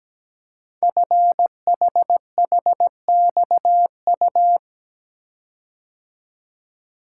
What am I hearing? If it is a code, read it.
Morse code, heard as a keyed tone, about 17 wpm, 700 Hz. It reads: FHHXU
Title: Code Groups - 5 Letters - 17wpm